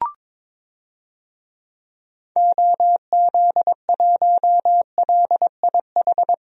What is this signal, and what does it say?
Morse code, heard as a keyed tone, about 22 wpm, 700 Hz. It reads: OZ1LIH
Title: DXCC-100 Call Signs - 6 Characters - 22wpm